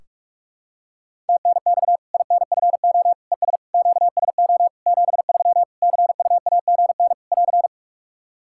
Morse code, heard as a keyed tone, about 41 wpm, 700 Hz. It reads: TNX INFO ES QSO 73 CUAGN <AR>
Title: Practice QSO - Exchange 4 - Ending - 45wpm